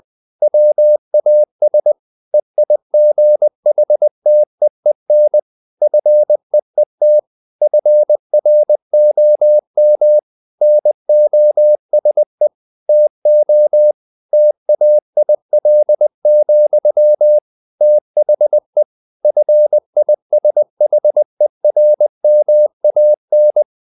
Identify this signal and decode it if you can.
Morse code, heard as a keyed tone, about 20 wpm, 600 Hz. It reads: WAS EIGHTEEN FEET FROM NOSE TO TAIL, THE FISHERMAN
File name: the old man and the sea part 2 20wpm